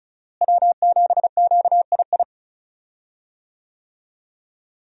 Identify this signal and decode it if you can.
Morse code, heard as a keyed tone, about 34 wpm, 700 Hz. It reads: W7QII
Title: USA Call Signs - 5 Characters - Repeated 2x - 35wpm